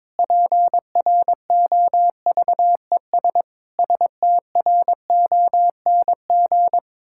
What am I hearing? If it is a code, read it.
Morse code, heard as a keyed tone, about 22 wpm, 700 Hz. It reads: PROVES STRONG
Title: Pangram Challenge v2 - 22wpm